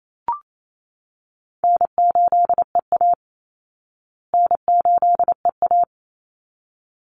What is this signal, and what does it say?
Morse code, heard as a keyed tone, about 28 wpm, 700 Hz. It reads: N8EA N8EA
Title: Call Signs Repeated 3x - 28wpm